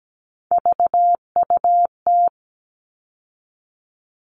Morse code, heard as a keyed tone, about 17 wpm, 700 Hz. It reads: VUT
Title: All Cut Numbers - 3-Digit 17wpm